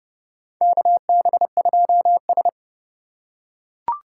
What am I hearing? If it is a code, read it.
Morse code, heard as a keyed tone, about 30 wpm, 700 Hz. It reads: KB2S
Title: K1USN SST QSO Practice - 30wpm